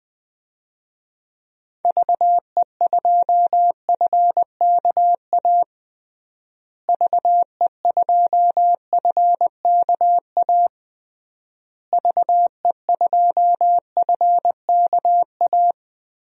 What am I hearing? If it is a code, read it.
Morse code, heard as a keyed tone, about 20 wpm, 700 Hz. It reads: VE2FKA VE2FKA VE2FKA
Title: DXCC-100 Call Signs - 6 Characters - Repeated 3x - 20wpm